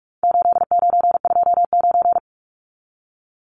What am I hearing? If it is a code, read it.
Morse code, heard as a keyed tone, about 44 wpm, 700 Hz. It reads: 8929
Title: Numbers - 4-Digit 45wpm